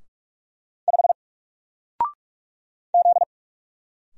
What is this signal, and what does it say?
Morse code, heard as a keyed tone, about 46 wpm, 700 Hz. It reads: 5 Z
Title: Single Letter-Number 45wpm